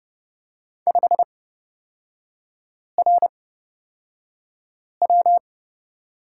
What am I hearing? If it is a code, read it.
Morse code, heard as a keyed tone, about 30 wpm, spaced as for 5 wpm, 700 Hz. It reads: 5RW
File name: Lesson 37e - Letter x - Three Characters 30wpm